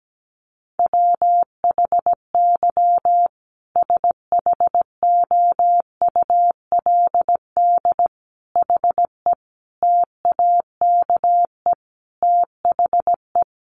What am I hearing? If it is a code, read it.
Morse code, heard as a keyed tone, about 17 wpm, 700 Hz. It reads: WHY SHOULD HE TAKE THE
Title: Sentences from Top 1500 Words 17wpm